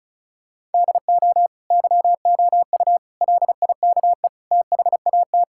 Morse code, heard as a keyed tone, about 34 wpm, 700 Hz. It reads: DO YOU LIKE THAT
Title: Sentences from Top 1000 Words - Review Part 5 - 35wpm